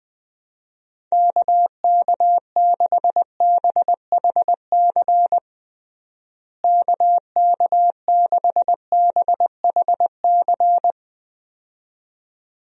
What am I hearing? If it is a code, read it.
Morse code, heard as a keyed tone, about 20 wpm, 700 Hz. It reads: KK6BHC KK6BHC
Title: USA Call Signs - 6 Characters - Repeated 2x - 20wpm